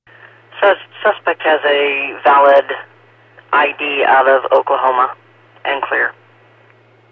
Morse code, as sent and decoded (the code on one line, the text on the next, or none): none